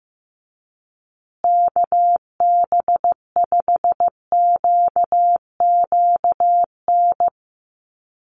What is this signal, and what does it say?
Morse code, heard as a keyed tone, about 15 wpm, 700 Hz. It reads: KB5QQN
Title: USA Call Signs Repeated 3x 15wpm